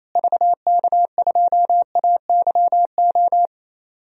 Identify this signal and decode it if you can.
Morse code, heard as a keyed tone, about 28 wpm, 700 Hz. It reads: VK2AYO
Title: DXCC-100 Call Signs - 6 Characters - Repeated 3x - 28wpm